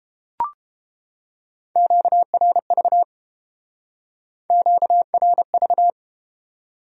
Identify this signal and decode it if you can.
Morse code, heard as a keyed tone, about 33 wpm, 700 Hz. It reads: QRV QRV